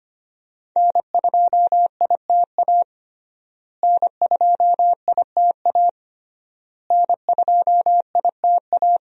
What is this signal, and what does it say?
Morse code, heard as a keyed tone, about 25 wpm, 700 Hz. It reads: N2ITA N2ITA N2ITA
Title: USA Call Signs - 5 Characters - Repeated 3x - 25wpm